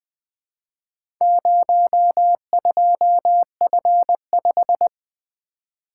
Morse code, heard as a keyed tone, about 20 wpm, 700 Hz. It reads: 02F5
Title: Four Letter-Number Combinations - 20wpm